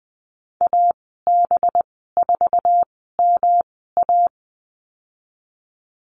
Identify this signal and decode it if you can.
Morse code, heard as a keyed tone, about 21 wpm, spaced as for 15 wpm, 700 Hz. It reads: AB4MA
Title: 193 Lesson 31g - Letter v - Call Signs 20wpm